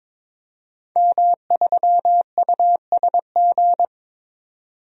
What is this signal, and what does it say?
Morse code, heard as a keyed tone, about 22 wpm, 700 Hz. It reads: M3USG
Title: DXCC-100 Call Signs - 5 Characters - 22wpm